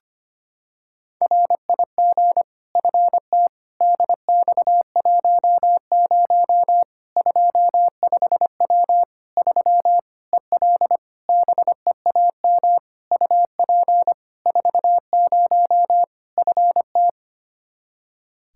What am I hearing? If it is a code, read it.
Morse code, heard as a keyed tone, about 25 wpm, 700 Hz. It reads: RIG FT DX10 25W 3 EL BEAM UP 40 FT
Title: Practice QSO - Exchange 2 - Info - 25wpm